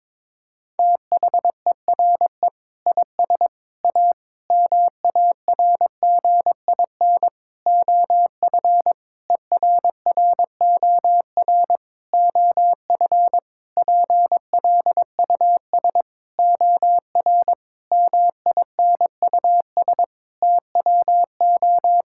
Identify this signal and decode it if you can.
Morse code, heard as a keyed tone, about 22 wpm, 700 Hz. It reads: THERE IS A MARGIN OF ERROR OF PLUS OR MINUS TWO
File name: practice / 22wpm / Sentences with 8-Word Phrases - 22wpm